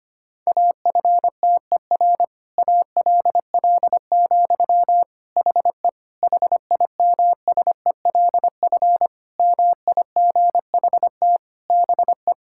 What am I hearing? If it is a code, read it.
Morse code, heard as a keyed tone, about 25 wpm, 700 Hz. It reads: AFTER ALL, HE HIMSELF MIGHT BE
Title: Sentences from Top 1600 Words 25wpm